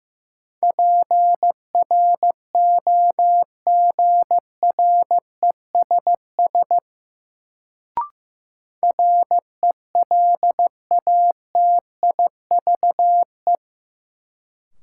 Morse code, heard as a keyed tone, about 16 wpm, 700 Hz. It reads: PROGRESS RELATIVE
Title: Sentences from Top 1400 Words - New Word Drill - 15wpm